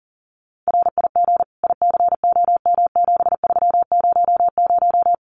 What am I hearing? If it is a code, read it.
Morse code, heard as a keyed tone, about 40 wpm, 700 Hz. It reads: RIG ICOM7300
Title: Common QSO Phrases 40wpm